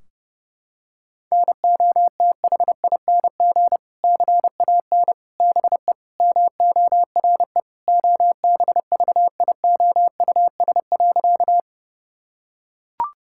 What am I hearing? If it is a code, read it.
Morse code, heard as a keyed tone, about 30 wpm, 700 Hz. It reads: NOTHING CAN BE MORE OBVIOUS.
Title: Sentences from Top 1600 Words 30wpm